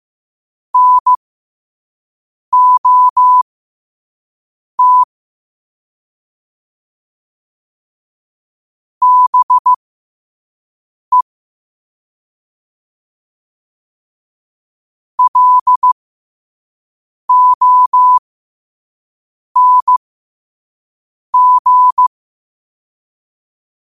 Morse code, heard as a keyed tone, about 15 wpm, spaced as for 5 wpm, 1000 Hz. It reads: NOT BE LONG
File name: qotdfiles / 5wpm